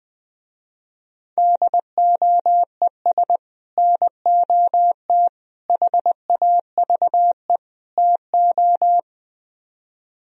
Morse code, heard as a keyed tone, about 20 wpm, 700 Hz. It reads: DOES NOT HAVE TO